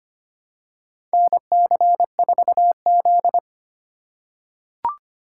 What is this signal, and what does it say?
Morse code, heard as a keyed tone, about 25 wpm, 700 Hz. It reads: NC4Z
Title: USA Call Signs - 4 Characters - 4x Character Spacing - 25wpm